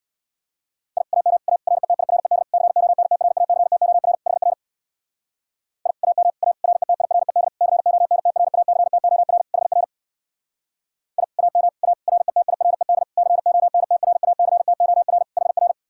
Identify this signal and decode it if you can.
Morse code, heard as a keyed tone, about 65 wpm, 700 Hz. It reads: I AM A RETIRED CONTRACTOR HR I AM A RETIRED CONTRACTOR HR I AM A RETIRED CONTRACTOR HR